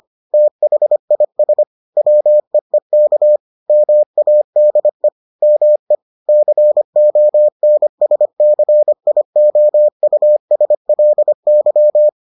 Morse code, heard as a keyed tone, about 25 wpm, 600 Hz. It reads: THIS WEEK MADE ME CONSCIOUSLY